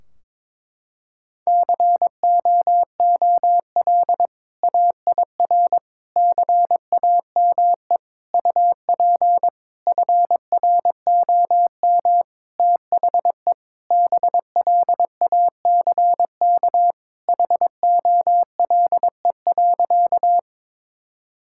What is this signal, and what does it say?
Morse code, heard as a keyed tone, about 21 wpm, 700 Hz. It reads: COOL AIR CAME UP FROM THE BLACK HOLE.
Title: Sentences from Top 1600 Words 22wpm